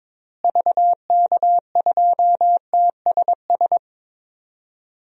Morse code, heard as a keyed tone, about 21 wpm, 700 Hz. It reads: VK2TSS
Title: DXCC-100 Call Signs - 6 Characters - Repeated 2x - 22wpm